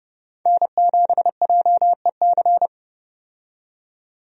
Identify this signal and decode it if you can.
Morse code, heard as a keyed tone, about 32 wpm, 700 Hz. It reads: N7JEC
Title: USA Call Signs - 5 Characters - 30wpm